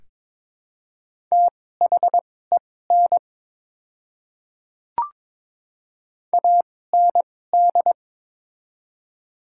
Morse code, heard as a keyed tone, about 22 wpm, spaced as for 16 wpm, 700 Hz. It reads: THEN AND
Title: 2x Character Spacing - Top 100 Words - 22wpm